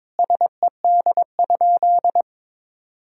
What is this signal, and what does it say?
Morse code, heard as a keyed tone, about 22 wpm, 700 Hz. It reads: SED?
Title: Sentences from Top 100 Words and Abbreviations 22wpm